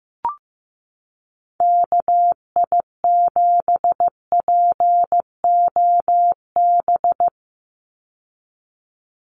Morse code, heard as a keyed tone, about 15 wpm, 700 Hz. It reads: KI7POB
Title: USA Call Signs 15wpm